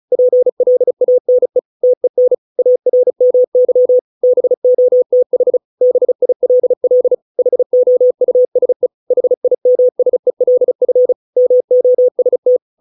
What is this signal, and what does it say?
Morse code, heard as a keyed tone, about 35 wpm, 500 Hz. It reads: PLANE TEN ARMY BOTH BILL HOUSE HIMSELF MOST